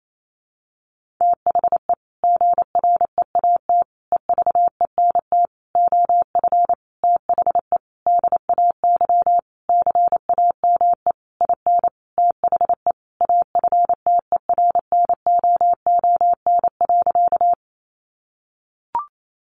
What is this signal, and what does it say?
Morse code, heard as a keyed tone, about 28 wpm, 700 Hz. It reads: THE GREAT EVENT OF THE DAY CAME IN THE AFTERNOON.
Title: Sentences from Top 1500 Words 28wpm